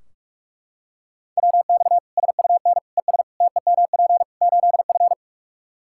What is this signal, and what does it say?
Morse code, heard as a keyed tone, about 41 wpm, 700 Hz. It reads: WX SUN ES TEMP 8F
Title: Practice QSO - Exchange 2 - Info - 45wpm